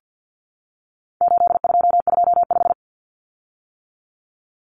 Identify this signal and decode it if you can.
Morse code, heard as a keyed tone, about 50 wpm, 700 Hz. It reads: 8225
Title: Common Cut Numbers - 4-Digit 50wpm